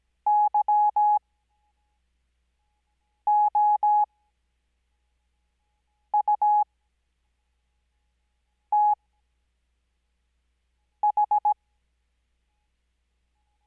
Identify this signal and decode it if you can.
Morse code, heard as a keyed tone, about 17 wpm, spaced as for 4 wpm, 840 Hz. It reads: YOUTH
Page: Farnsworth Spacing: 15WPM Character speed Spaced at 5WPM.